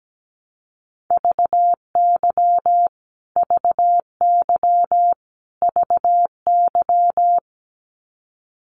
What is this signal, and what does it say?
Morse code, heard as a keyed tone, about 17 wpm, 700 Hz. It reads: VY VY VY
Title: Abbreviations Repeated 3x - 17wpm